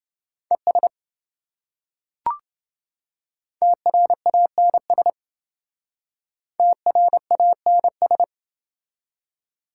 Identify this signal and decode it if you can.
Morse code, heard as a keyed tone, about 28 wpm, 700 Hz. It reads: ES TRANS TRANS